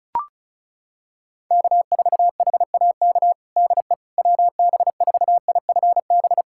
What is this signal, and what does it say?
Morse code, heard as a keyed tone, about 36 wpm, 700 Hz. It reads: K4HAK DE WB4IFB
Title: USA Callsign DE Callsign 35wpm